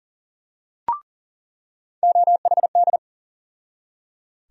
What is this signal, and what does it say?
Morse code, heard as a keyed tone, about 37 wpm, 700 Hz. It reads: OHD
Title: USA Call Sign Postfixes - 3 Letters 40wpm